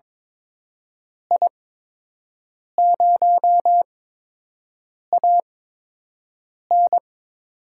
Morse code, heard as a keyed tone, about 22 wpm, spaced as for 6 wpm, 700 Hz. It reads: I0AN
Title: USA Call Signs - 8x Character Spacing - 22wpm